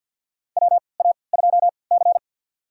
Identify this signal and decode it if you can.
Morse code, heard as a keyed tone, about 52 wpm, spaced as for 29 wpm, 700 Hz. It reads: WA2C
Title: USA Call Signs - 4 Characters - 3x Character Spacing - 50wpm